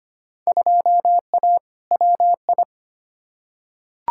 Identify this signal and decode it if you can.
Morse code, heard as a keyed tone, about 25 wpm, 700 Hz. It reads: 2A WI
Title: Field Day Exchanges Encore - 25wpm